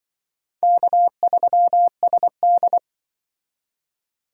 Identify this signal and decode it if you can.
Morse code, heard as a keyed tone, about 24 wpm, 700 Hz. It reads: K3SD
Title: Speed Racing - USA 1x2 Call Signs 20wpm